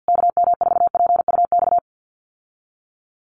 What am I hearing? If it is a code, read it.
Morse code, heard as a keyed tone, about 50 wpm, 700 Hz. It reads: KM4PUX
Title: USA Call Signs Repeated 3x 50wpm